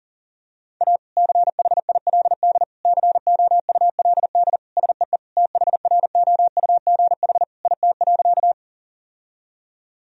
Morse code, heard as a keyed tone, about 40 wpm, 700 Hz. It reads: A CHILD COULD SEE THROUGH IT.
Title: Sentences from Top 1000 Words - Review Part 1 - 40wpm